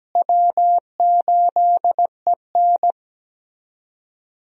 Morse code, heard as a keyed tone, about 17 wpm, 700 Hz. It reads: W8EN
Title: USA Call Signs - 4 Characters - Repeated 3x - 17wpm